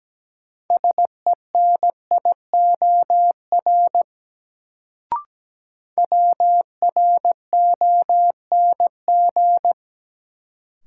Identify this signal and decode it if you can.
Morse code, heard as a keyed tone, about 17 wpm, 700 Hz. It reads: SENIOR WRONG
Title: Sets of 5 Words Encore - New Word Drill - 17wpm